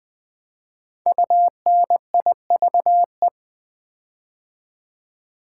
Morse code, heard as a keyed tone, about 20 wpm, 700 Hz. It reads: UNIVE
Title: Build A Word - 5 Letters At A Time - 10-Letter Words - 20wpm